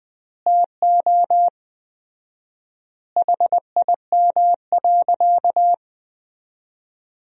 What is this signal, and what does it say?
Morse code, heard as a keyed tone, about 21 wpm, 700 Hz. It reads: TO HIM.
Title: 4x Word-Spacing - Sentences from Top 300 Words - 20wpm